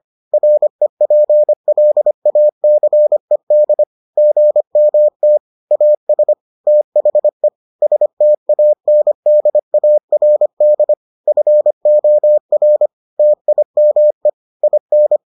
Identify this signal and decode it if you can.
Morse code, heard as a keyed tone, about 25 wpm, 600 Hz. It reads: REPLACED GMT AS THE STANDARD FOR TIME IN